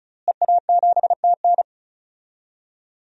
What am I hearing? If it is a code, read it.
Morse code, heard as a keyed tone, about 36 wpm, 700 Hz. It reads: EA7TN
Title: DXCC-100 Call Signs - 5 Characters - Repeated 2x - 35wpm